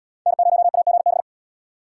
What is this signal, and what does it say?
Morse code, heard as a keyed tone, about 77 wpm, 700 Hz. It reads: N0AGB